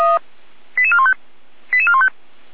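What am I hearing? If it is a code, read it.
Five-tone selective call (ZVEI), heard as quick runs of five tones, tones of about 70 ms. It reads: 88415 88415
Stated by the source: ZVEI five-tone selective call received with an SDR.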